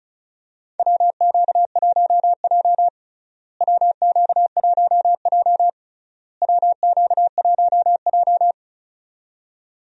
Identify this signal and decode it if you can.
Morse code, heard as a keyed tone, about 34 wpm, 700 Hz. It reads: WQ1J WQ1J WQ1J
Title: USA Call Signs - 4 Characters - Repeated 3x - 35wpm